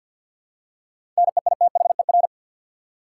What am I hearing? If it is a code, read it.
Morse code, heard as a keyed tone, about 53 wpm, 700 Hz. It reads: NEITHER
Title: Seven Letter Words - 50wpm